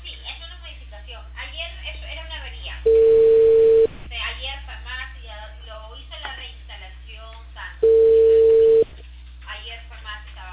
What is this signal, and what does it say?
Ringback tone (China, 450 Hz), with speech.